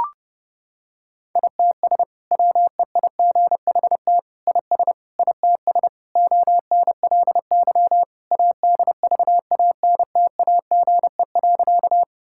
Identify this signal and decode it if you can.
Morse code, heard as a keyed tone, about 30 wpm, 700 Hz. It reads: ITS WEIGHT IS ITS ONLY ADVANTAGE.
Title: Sentences from Top 1000 Words - Review Part 6 - 30wpm